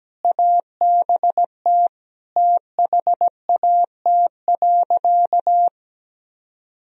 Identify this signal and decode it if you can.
Morse code, heard as a keyed tone, about 17 wpm, 700 Hz. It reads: ABT THAT.
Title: Sentences from Top 400 Words and Abbreviations 17wpm